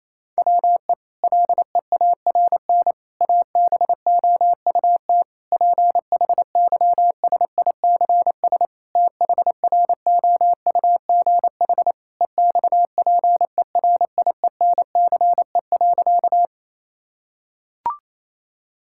Morse code, heard as a keyed tone, about 28 wpm, 700 Hz. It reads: WE LEARN ABOUT PHYSICS THROUGH EXPERIENCE.
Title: Sentences from Top 1000 Words - Review Part 5 - 28wpm